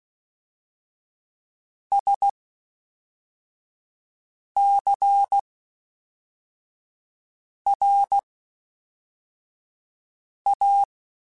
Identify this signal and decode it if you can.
Morse code, heard as a keyed tone, about 16 wpm, spaced as for 4 wpm, 800 Hz. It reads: SCRA
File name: Special INGs / Special INGs-mp3s / Fast ing @10